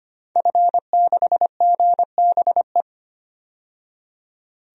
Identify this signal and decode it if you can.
Morse code, heard as a keyed tone, about 25 wpm, 700 Hz. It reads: F6GBE
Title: DXCC-100 Call Signs - 5 Characters - 25wpm